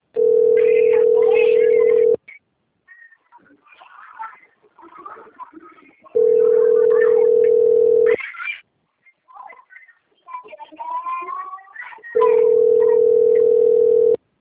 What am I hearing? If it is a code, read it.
Ringback tone (North America), 440 + 480 Hz.
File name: rg-1600-unknown- - - .39175